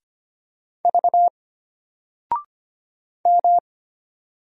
Morse code, heard as a keyed tone, about 25 wpm, 700 Hz. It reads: V M
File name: Single-Letter-Number-25wpm